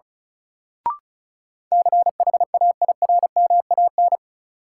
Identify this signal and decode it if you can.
Morse code, heard as a keyed tone, about 36 wpm, 700 Hz. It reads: CHAIRMAN
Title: Sets of 3 Words - New Word Drill - 35wpm